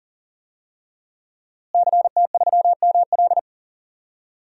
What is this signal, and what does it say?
Morse code, heard as a keyed tone, about 40 wpm, 700 Hz. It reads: CT3ML
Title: DXCC-100 Call Signs - 5 Characters - Repeated 3x - 40wpm